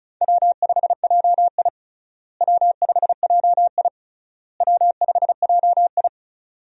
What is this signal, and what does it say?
Morse code, heard as a keyed tone, about 36 wpm, 700 Hz. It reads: W5JI W5JI W5JI
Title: USA Call Signs - 4 Characters - Repeated 3x - 35wpm